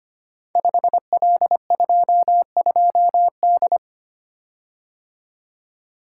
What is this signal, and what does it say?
Morse code, heard as a keyed tone, about 25 wpm, 690 Hz. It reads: 5L22D
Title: Code Groups - 5 Letter:Numbers - 25wpm